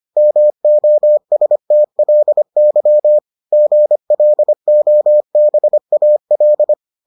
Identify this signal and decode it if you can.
Morse code, heard as a keyed tone, about 25 wpm, 600 Hz. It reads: MOSTLY GLOBAL